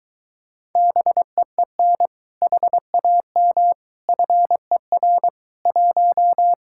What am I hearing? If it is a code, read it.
Morse code, heard as a keyed tone, about 22 wpm, 700 Hz. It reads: BEEN HAM FER 1